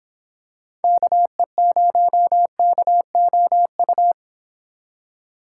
Morse code, heard as a keyed tone, about 25 wpm, 700 Hz. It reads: KE0KOU